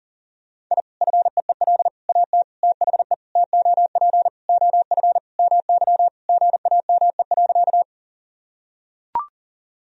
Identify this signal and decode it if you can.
Morse code, heard as a keyed tone, about 40 wpm, 700 Hz. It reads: I FEEL AT THE TOP OF MY GAME.
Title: Sentences from Top 1000 Words - Review Part 2 - 40wpm